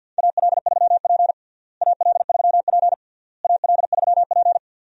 Morse code, heard as a keyed tone, about 51 wpm, 700 Hz. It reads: AL3P AL3P AL3P
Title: USA Call Signs - 4 Characters - Repeated 3x - 50wpm